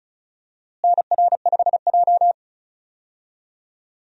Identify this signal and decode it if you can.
Morse code, heard as a keyed tone, about 36 wpm, 700 Hz. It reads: NR5J
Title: USA Call Signs - 4 Characters - Repeated 2x - 35wpm